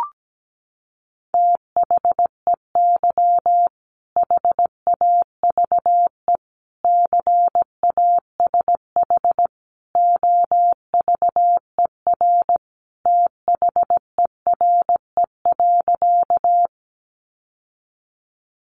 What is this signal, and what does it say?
Morse code, heard as a keyed tone, about 17 wpm, 700 Hz. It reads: THEY HAVE CASH OVER THERE.
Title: Sentences from Top 700 Words 17wpm